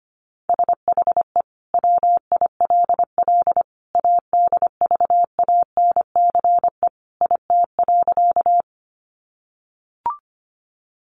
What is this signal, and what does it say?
Morse code, heard as a keyed tone, about 25 wpm, 700 Hz. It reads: SHE WILL ADVANCE IT.
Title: Sentences from Top 1400 Words 25wpm